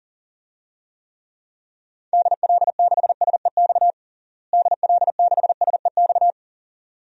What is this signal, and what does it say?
Morse code, heard as a keyed tone, about 40 wpm, 700 Hz. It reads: DL6SEX DL6SEX
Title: DXCC-100 Call Signs - 6 Characters - Repeated 2x - 40wpm